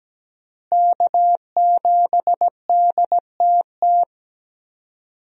Morse code, heard as a keyed tone, about 17 wpm, 700 Hz. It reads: K7DTT
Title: USA Call Signs - 5 Characters - Repeated 3x - 17wpm